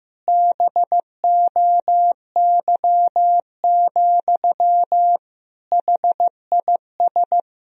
Morse code, heard as a keyed tone, about 15 wpm, 700 Hz. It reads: BOY, HIS
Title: Sentences from Top 1000 Words - Review Part 3 - 15wpm